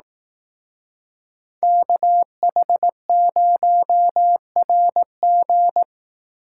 Morse code, heard as a keyed tone, about 18 wpm, 700 Hz. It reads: KH0RG
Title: Speed Racing - USA 2x2 Call Signs 15wpm